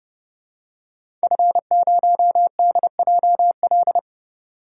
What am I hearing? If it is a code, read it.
Morse code, heard as a keyed tone, about 30 wpm, 700 Hz. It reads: F0DJL
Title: DXCC-100 Call Signs - 5 Characters - 30wpm